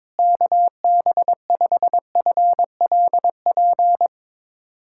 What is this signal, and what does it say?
Morse code, heard as a keyed tone, about 22 wpm, 700 Hz. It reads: KB5FLP
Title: USA Callsign DE Callsign - 6 Characters - 22wpm